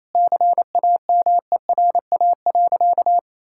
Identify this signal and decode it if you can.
Morse code, heard as a keyed tone, about 27 wpm, 700 Hz. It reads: CAMERA.
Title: Sentences from Top 1000 Words - Review Part 5 - 28wpm